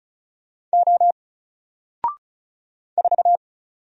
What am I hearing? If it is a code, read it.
Morse code, heard as a keyed tone, about 36 wpm, 700 Hz. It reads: O 4
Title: Single Letter-Number 35wpm